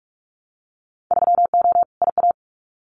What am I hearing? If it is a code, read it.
Morse code, heard as a keyed tone, about 46 wpm, 700 Hz. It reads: 3O IA
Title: Winter Field Day Exchanges 45wpm